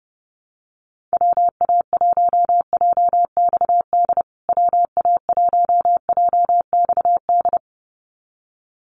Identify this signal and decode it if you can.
Morse code, heard as a keyed tone, about 30 wpm, 700 Hz. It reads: WA1JXD WA1JXD
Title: USA Call Signs Repeated 2x 30wpm